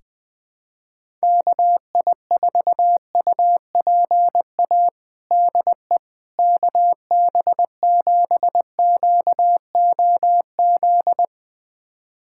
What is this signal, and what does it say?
Morse code, heard as a keyed tone, about 20 wpm, 700 Hz. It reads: KI4UPA DE KB7QOZ
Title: USA Callsign DE Callsign 20wpm